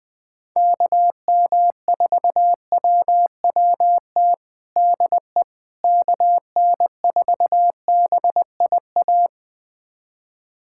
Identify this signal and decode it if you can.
Morse code, heard as a keyed tone, about 20 wpm, 700 Hz. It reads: KM4WWT DE KN4BIA
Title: USA Callsign DE Callsign 20wpm